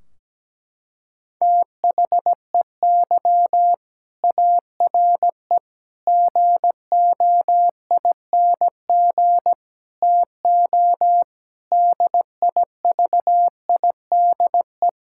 Morse code, heard as a keyed tone, about 17 wpm, 700 Hz. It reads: THEY ARE GOING TO DIVIDE
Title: Sentences from Top 1600 Words 17wpm